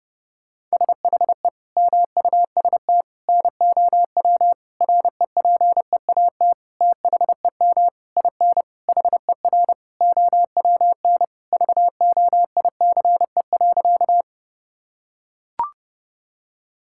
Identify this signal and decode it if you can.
Morse code, heard as a keyed tone, about 28 wpm, 700 Hz. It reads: SHE MUST NOW REPEAT THEM IN HER OWN VOICE.
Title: Sentences from Top 1600 Words 30wpm